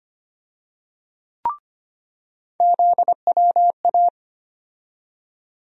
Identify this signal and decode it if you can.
Morse code, heard as a keyed tone, about 25 wpm, 700 Hz. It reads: ZWA
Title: USA Call Sign Postfixes - 3 Letters 25wpm